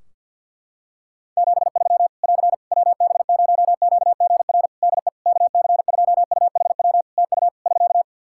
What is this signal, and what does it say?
Morse code, heard as a keyed tone, about 50 wpm, 700 Hz. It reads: 73 <AR> WB0QGR DE KC2UHW TU <SK>
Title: Practice QSO - Exchange 4 - Ending - 50wpm